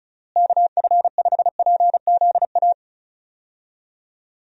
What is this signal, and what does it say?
Morse code, heard as a keyed tone, about 34 wpm, 700 Hz. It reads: KF5PZA
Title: USA Call Signs - 6 Characters - Repeated 3x - 35wpm